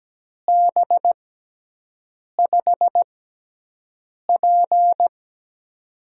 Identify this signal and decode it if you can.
Morse code, heard as a keyed tone, about 17 wpm, spaced as for 6 wpm, 700 Hz. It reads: B5P
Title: USA Call Signs - 6x Character Spacing - 17wpm